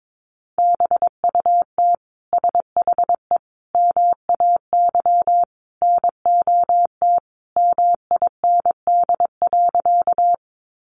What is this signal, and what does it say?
Morse code, heard as a keyed tone, about 22 wpm, 700 Hz. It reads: BUT SHE MAY NOT MIND.
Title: Sentences from Top 600 Words 22wpm